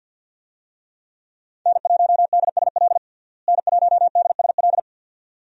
Morse code, heard as a keyed tone, about 50 wpm, 700 Hz. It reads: N1DSL N1DSL
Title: USA Call Signs - 5 Characters - Repeated 2x - 50wpm